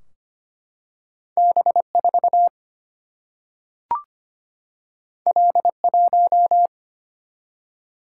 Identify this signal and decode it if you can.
Morse code, heard as a keyed tone, about 25 wpm, 700 Hz. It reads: B4 L1
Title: Single-Letter Single-Number 25wpm